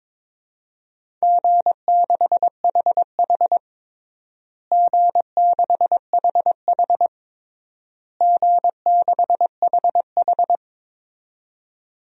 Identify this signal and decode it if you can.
Morse code, heard as a keyed tone, about 21 wpm, 700 Hz. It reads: G6HH G6HH G6HH
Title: DXCC-100 Call Signs - 4 Characters - Repeated 3x - 22wpm